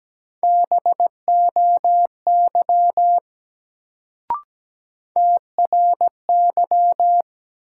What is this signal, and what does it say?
Morse code, heard as a keyed tone, about 17 wpm, 700 Hz. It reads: BOY TRY
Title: Sentences from Top 200 Words - New Word Drill - 17wpm